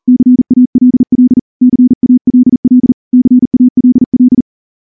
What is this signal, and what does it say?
Morse code, heard as a keyed tone, about 38 wpm, 250 Hz. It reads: CALL CALL CALL
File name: morse-training-40WPM